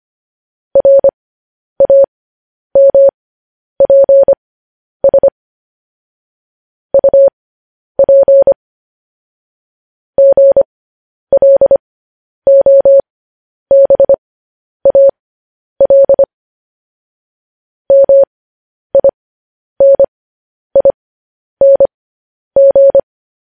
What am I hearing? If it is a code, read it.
Morse code, heard as a keyed tone, about 25 wpm, spaced as for 10 wpm, 550 Hz. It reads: RAMPS UP GLOBAL MINING
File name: podcast / News headlines-1x-25x10